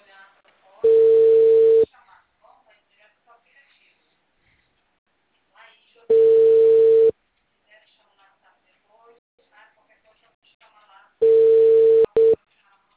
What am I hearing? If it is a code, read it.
Ringback tone (China, 450 Hz), with speech.
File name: out- -2004